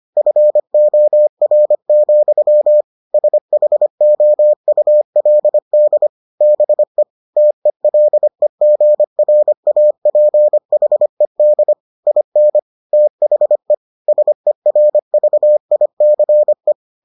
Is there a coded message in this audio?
Yes: Morse code from 0.2 to 16.7 s, heard as a keyed tone, about 25 wpm, 600 Hz. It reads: FOR, SHOULD BE TELEGRAPHED IN THE SERVICE